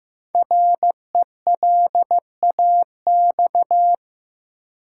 Morse code, heard as a keyed tone, about 15 wpm, 700 Hz. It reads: RELAX
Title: Sentences from Top 1400 Words - New Word Drill - 15wpm